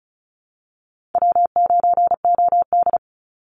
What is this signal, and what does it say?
Morse code, heard as a keyed tone, about 34 wpm, 700 Hz. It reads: W9OD
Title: Easy Call Signs Repeated 2x - 35wpm